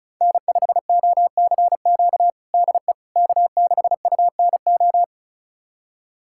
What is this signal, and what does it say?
Morse code, heard as a keyed tone, about 34 wpm, 700 Hz. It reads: N5OCQ DE K6UNO
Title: USA Callsign DE Callsign - 5 Characters - 35wpm